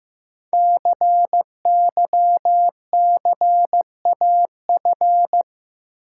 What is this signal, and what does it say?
Morse code, heard as a keyed tone, about 15 wpm, 700 Hz. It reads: CYCAF
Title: Code Groups - 5 Letters - 15wpm